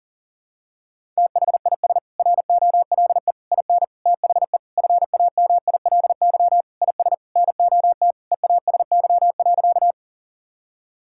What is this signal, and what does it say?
Morse code, heard as a keyed tone, about 40 wpm, 700 Hz. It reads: THIS ROLE IN THE FAMILY IS NOT EASY.
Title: Sentences from Top 1000 Words - Review Part 6 - 40wpm